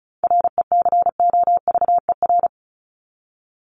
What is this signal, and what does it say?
Morse code, heard as a keyed tone, about 34 wpm, 700 Hz. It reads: RECOVER
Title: Top 1751-2000 Words 35wpm